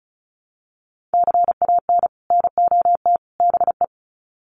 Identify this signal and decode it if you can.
Morse code, heard as a keyed tone, about 34 wpm, 700 Hz. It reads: CAN NOT BE
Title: Sets of 3 Words 35wpm